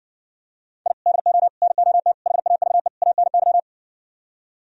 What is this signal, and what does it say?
Morse code, heard as a keyed tone, about 69 wpm, 700 Hz. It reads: I DO NOT HAVE ANY